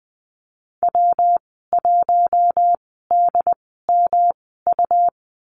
Morse code, heard as a keyed tone, about 20 wpm, spaced as for 14 wpm, 700 Hz. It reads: W1DMU
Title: 122 Lesson 20g - Number 3 - Call Signs 20wpm